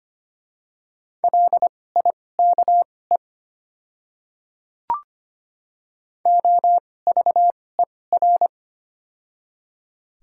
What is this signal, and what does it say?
Morse code, heard as a keyed tone, about 25 wpm, spaced as for 18 wpm, 700 Hz. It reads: LIKE OVER
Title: 2x Character Spacing - Top 100 Words - 25wpm